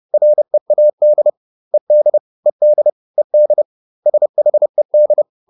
Morse code, heard as a keyed tone, about 30 wpm, 600 Hz. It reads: READ ED ED ED SHED
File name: ED4 30